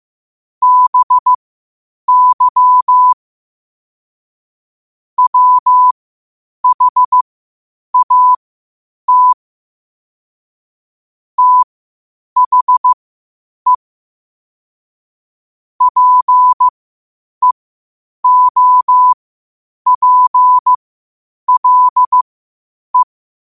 Morse code, heard as a keyed tone, about 15 wpm, spaced as for 8 wpm, 1000 Hz. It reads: BY WHAT THE PEOPLE
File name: qotdfiles / 8wpm